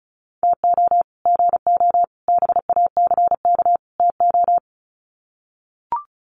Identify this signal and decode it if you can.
Morse code, heard as a keyed tone, about 34 wpm, 700 Hz. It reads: TO GO BACK TO
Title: Sets of 4 Words 35wpm